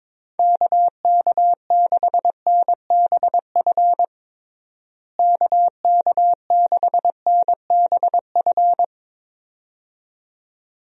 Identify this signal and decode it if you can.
Morse code, heard as a keyed tone, about 23 wpm, 700 Hz. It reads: KK6NBF KK6NBF
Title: USA Call Signs - 6 Characters - Repeated 2x - 22wpm